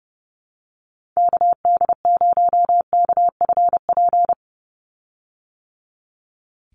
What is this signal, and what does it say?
Morse code, heard as a keyed tone, about 30 wpm, 700 Hz. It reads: KD0KFP
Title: USA Call Signs 30wpm